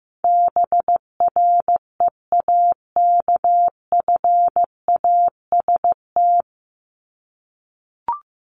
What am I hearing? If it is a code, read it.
Morse code, heard as a keyed tone, about 15 wpm, 700 Hz. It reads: BREAKFAST
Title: Top 1251-1500 Words 15wpm